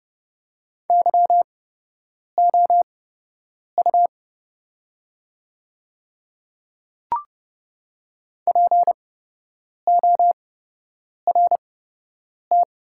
Morse code, heard as a keyed tone, about 32 wpm, spaced as for 8 wpm, 700 Hz. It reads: YOU PORT
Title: 8x Character Spacing - Top 200 Words - 30wpm